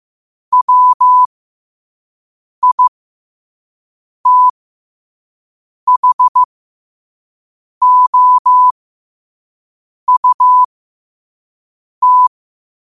Morse code, heard as a keyed tone, about 15 wpm, spaced as for 5 wpm, 1000 Hz. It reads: WITHOUT
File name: qotdfiles / 5wpm